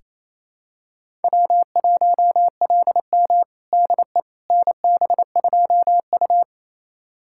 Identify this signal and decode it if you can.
Morse code, heard as a keyed tone, about 28 wpm, 700 Hz. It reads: W1LM DE NB2U
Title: USA Callsign DE Callsign - 4 Characters - 28wpm